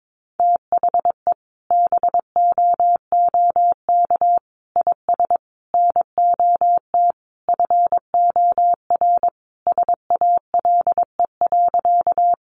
Morse code, heard as a keyed tone, about 22 wpm, 700 Hz. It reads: THE BOOK IS NOT FOR SALE.
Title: Sentences from Top 1400 Words 22wpm